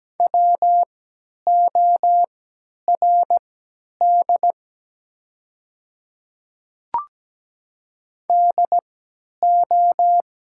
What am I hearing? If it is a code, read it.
Morse code, heard as a keyed tone, about 17 wpm, spaced as for 10 wpm, 700 Hz. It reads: WORD DO
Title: 3x Character Spacing - Top 100 Words - 17wpm